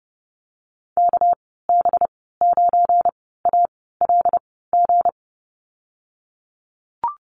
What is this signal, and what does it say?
Morse code, heard as a keyed tone, about 30 wpm, spaced as for 17 wpm, 700 Hz. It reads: KB9ALG
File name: Lesson 37g - Letter x - Call Signs 30wpm